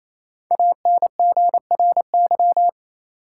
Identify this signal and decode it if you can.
Morse code, heard as a keyed tone, about 28 wpm, 700 Hz. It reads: ANGRY
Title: Five Letter Words - 28wpm